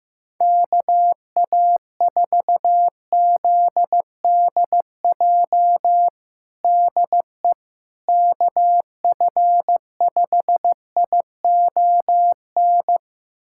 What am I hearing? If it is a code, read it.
Morse code, heard as a keyed tone, about 15 wpm, 700 Hz. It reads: KA4ZDJ DE KF5ION
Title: USA Callsign DE Callsign - 6 Characters - 15wpm